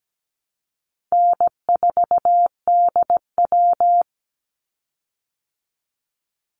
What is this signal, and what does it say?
Morse code, heard as a keyed tone, about 17 wpm, 700 Hz. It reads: N4DW
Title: Easy Call Signs Repeated 2x - 17wpm